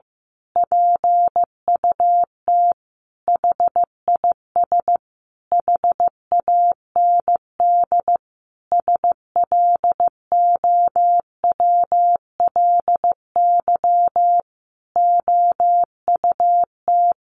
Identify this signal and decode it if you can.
Morse code, heard as a keyed tone, about 15 wpm, 700 Hz. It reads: PUT HIS HAND SLOWLY OUT
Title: A Sound Of Thunder 15wpm